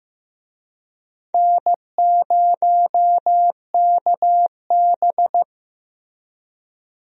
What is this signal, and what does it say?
Morse code, heard as a keyed tone, about 15 wpm, 700 Hz. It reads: N0KB
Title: USA Call Signs - 4 Characters - Repeated 3x - 15wpm